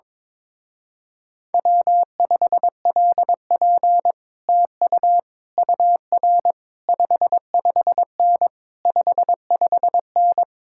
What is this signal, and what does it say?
Morse code, heard as a keyed tone, about 22 wpm, 700 Hz. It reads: W5LP TU UR 55N 55N
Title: Speed-Racing Common QSO Phrases - 22wpm